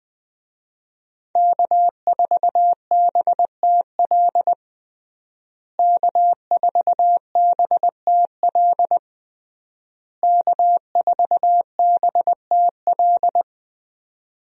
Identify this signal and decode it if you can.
Morse code, heard as a keyed tone, about 20 wpm, 700 Hz. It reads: K4BTL K4BTL K4BTL
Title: USA Call Signs - 5 Characters - Repeated 3x - 20wpm